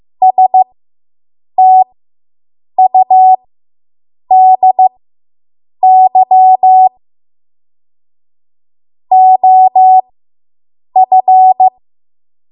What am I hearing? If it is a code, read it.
Morse code, heard as a keyed tone, about 15 wpm, spaced as for 7 wpm, 750 Hz. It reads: STUDY OF